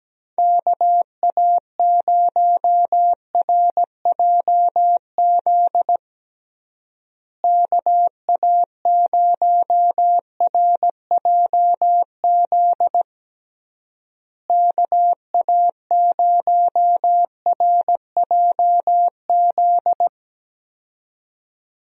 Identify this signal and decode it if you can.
Morse code, heard as a keyed tone, about 17 wpm, 700 Hz. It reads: KA0RJZ KA0RJZ KA0RJZ
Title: USA Call Signs - 6 Characters - Repeated 3x - 17wpm